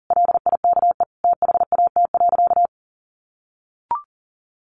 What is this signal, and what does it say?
Morse code, heard as a keyed tone, about 37 wpm, 700 Hz. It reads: LIKE THAT.
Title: A Sound Of Thunder 40wpm